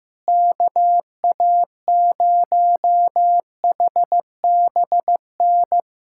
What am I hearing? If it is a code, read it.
Morse code, heard as a keyed tone, about 15 wpm, 700 Hz. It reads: KA0HBN
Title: USA Callsign DE Callsign - 6 Characters - 15wpm